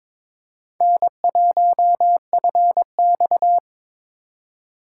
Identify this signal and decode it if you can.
Morse code, heard as a keyed tone, about 22 wpm, 700 Hz. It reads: N1FX
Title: USA Call Signs - 4 Characters - 3x Character Spacing - 22wpm